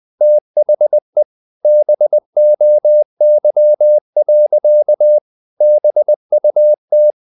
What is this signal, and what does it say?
Morse code, heard as a keyed tone, about 20 wpm, 600 Hz. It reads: THE BOY. BUT
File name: the old man and the sea / the old man and the sea part 1 20wpm